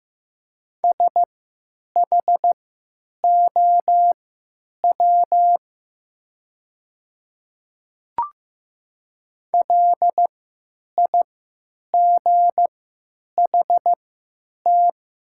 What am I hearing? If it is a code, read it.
Morse code, heard as a keyed tone, about 15 wpm, spaced as for 9 wpm, 700 Hz. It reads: SHOW LIGHT
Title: 3x Character Spacing - Top 300 Words - 15wpm